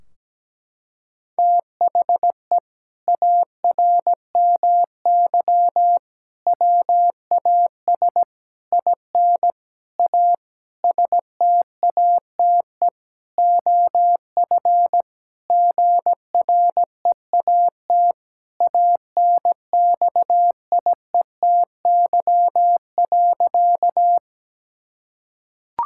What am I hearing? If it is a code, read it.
Morse code, heard as a keyed tone, about 17 wpm, 700 Hz. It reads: THE ARMY WAS IN A STATE OF GREAT ANXIETY.
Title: Sentences from Top 1600 Words 17wpm